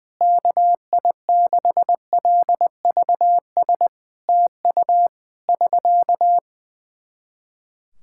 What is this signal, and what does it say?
Morse code, heard as a keyed tone, about 20 wpm, 700 Hz. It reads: KI6LVS TU <SK>
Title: Practice QSO - Exchange 4 - Ending - 20wpm